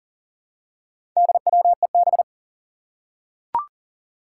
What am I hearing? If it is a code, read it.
Morse code, heard as a keyed tone, about 40 wpm, 700 Hz. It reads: DWEB
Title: Four Challenging Letter-Number Combinations 40wpm